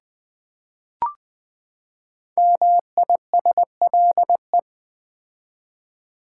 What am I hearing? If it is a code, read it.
Morse code, heard as a keyed tone, about 20 wpm, 700 Hz. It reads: MISLE
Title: Build A Word - 5 Letters At A Time - 10-Letter Words - 20wpm